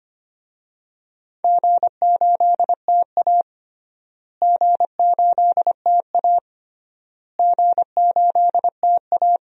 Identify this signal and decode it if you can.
Morse code, heard as a keyed tone, about 25 wpm, 700 Hz. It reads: G8TA G8TA G8TA
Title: DXCC-100 Call Signs - 4 Characters - Repeated 3x - 25wpm